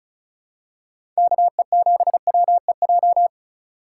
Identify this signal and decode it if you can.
Morse code, heard as a keyed tone, about 36 wpm, 700 Hz. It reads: KE7WEJ
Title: USA Call Signs - 6 Characters - 4x Character Spacing - 35wpm